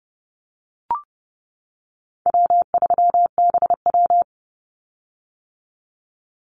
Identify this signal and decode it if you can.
Morse code, heard as a keyed tone, about 30 wpm, 700 Hz. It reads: W3BW
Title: Easy Call Signs - 30wpm